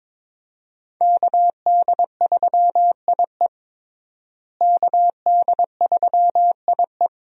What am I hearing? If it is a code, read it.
Morse code, heard as a keyed tone, about 21 wpm, 700 Hz. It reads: KD3IE KD3IE
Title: USA Call Signs - 5 Characters - Repeated 3x - 22wpm